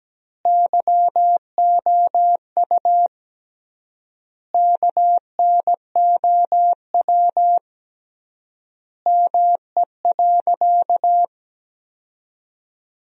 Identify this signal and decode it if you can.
Morse code, heard as a keyed tone, about 17 wpm, 700 Hz. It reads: YOU KNOW ME.
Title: 3x Word-Spacing - Sentences from Top 200 Words - 17wpm